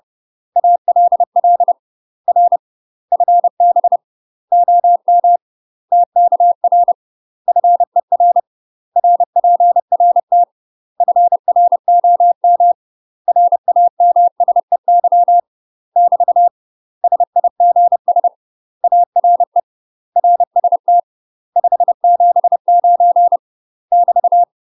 Morse code, heard as a keyed tone, about 30 wpm, 700 Hz. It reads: ALL R FB OM TKR FER RPRT FROM RAMSEY <BT> SIGS ARE RST 579 <BT>
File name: UK MASTER QSOs TRAINING / 30WPM / UK MASTER QSOs 0013